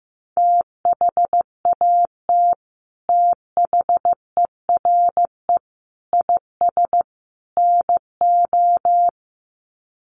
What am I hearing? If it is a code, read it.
Morse code, heard as a keyed tone, about 15 wpm, 700 Hz. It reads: THAT THERE IS NO
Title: Sets of 4 Words 15wpm